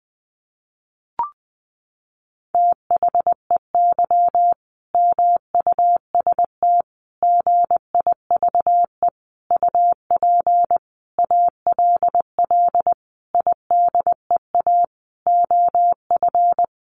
Morse code, heard as a keyed tone, about 20 wpm, 700 Hz. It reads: THEY MUST GIVE UP ALL IDEA OF
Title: Sentences from Top 1100 Words 20wpm